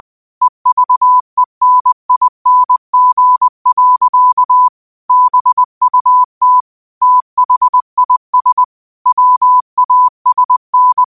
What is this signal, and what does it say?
Morse code, heard as a keyed tone, about 20 wpm, 1000 Hz. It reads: EVENING. BUT THIS WASN
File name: qotdfiles / 20wpm